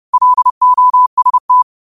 Morse code, heard as a keyed tone, about 30 wpm, 1000 Hz. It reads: LOST
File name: qotdfiles / 30wpm